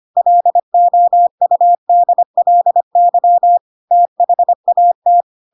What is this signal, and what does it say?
Morse code, heard as a keyed tone, about 25 wpm, 700 Hz. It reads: LOUDLY THAT
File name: Pinocchio / Pinocchio ch7 25 25 ik1hge